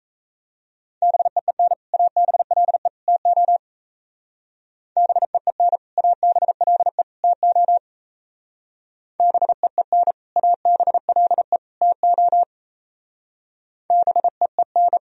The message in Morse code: BEEN ABLE TO BEEN ABLE TO BEEN ABLE TO BEEN

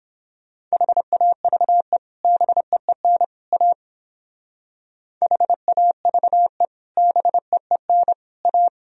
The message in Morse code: HAVE BEEN A HAVE BEEN A

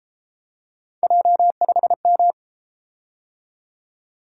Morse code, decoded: J5M

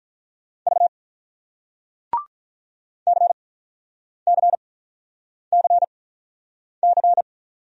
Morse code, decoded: V CCCC

